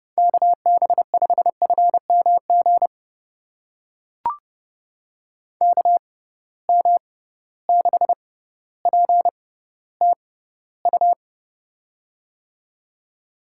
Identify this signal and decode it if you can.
Morse code, heard as a keyed tone, about 31 wpm, spaced as for 16 wpm, 700 Hz. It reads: KB5FMG KM6PTU
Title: USA Call Signs - 6 Characters - 6x Character Spacing - 30wpm